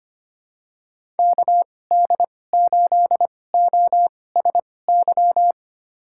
Morse code, heard as a keyed tone, about 25 wpm, spaced as for 18 wpm, 700 Hz. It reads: KD8OSY
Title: USA Call Signs - 6 Characters - 2x Character Spacing - 25wpm